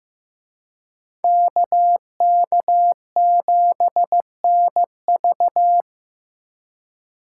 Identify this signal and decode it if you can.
Morse code, heard as a keyed tone, about 18 wpm, 700 Hz. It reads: KK7NV
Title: Speed Racing - USA Call Signs 15wpm